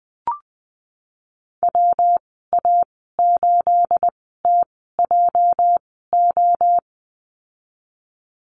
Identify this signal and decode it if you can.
Morse code, heard as a keyed tone, about 20 wpm, spaced as for 15 wpm, 700 Hz. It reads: WA8TJO
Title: 213 Lesson 34g - Number 8 - Call Signs 20wpm